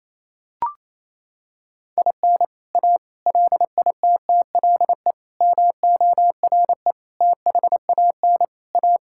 Morse code, heard as a keyed tone, about 28 wpm, 700 Hz. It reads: IN A LITTLE MORE THAN A
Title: Sentences from Top 1000 Words - Review Part 5 - 28wpm